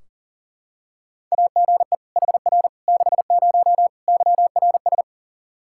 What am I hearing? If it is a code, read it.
Morse code, heard as a keyed tone, about 40 wpm, 700 Hz. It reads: AGE HR 60 YRS
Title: Practice QSO - Exchange 3 - More Info - 40wpm